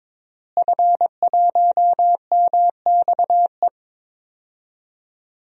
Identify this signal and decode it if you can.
Morse code, heard as a keyed tone, about 22 wpm, 700 Hz. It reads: F1MXE
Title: DXCC-100 Call Signs - 5 Characters - Repeated 3x - 22wpm